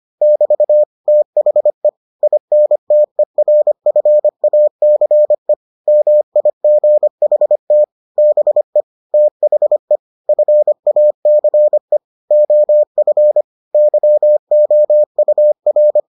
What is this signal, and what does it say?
Morse code, heard as a keyed tone, about 25 wpm, 600 Hz. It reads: <BT> THE INTERFACE MIGHT BE THE FACE OF YOUR